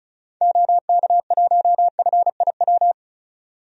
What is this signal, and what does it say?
Morse code, heard as a keyed tone, about 36 wpm, 700 Hz. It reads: OK1FIW
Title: DXCC-100 Call Signs - 6 Characters - 35wpm